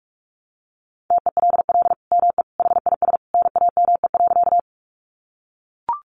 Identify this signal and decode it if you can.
Morse code, heard as a keyed tone, about 44 wpm, 700 Hz. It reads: TELL ME HIS NAME.
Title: Sentences from Top 200 Words 45wpm